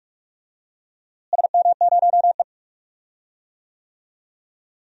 Morse code, heard as a keyed tone, about 43 wpm, 700 Hz. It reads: SM0E
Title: DXCC-100 Call Signs - 4 Characters - Repeated 2x - 45wpm